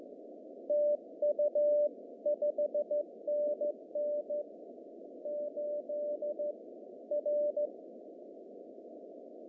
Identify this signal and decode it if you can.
Morse code, heard as a keyed tone, about 14 wpm, 590 Hz. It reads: TU 5NN 8 R
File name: CW / cqww-cw-k5tr